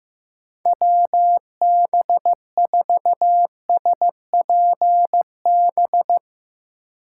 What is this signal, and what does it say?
Morse code, heard as a keyed tone, about 16 wpm, 700 Hz. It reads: WB4SPB
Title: K1USN SST QSO Practice - 15wpm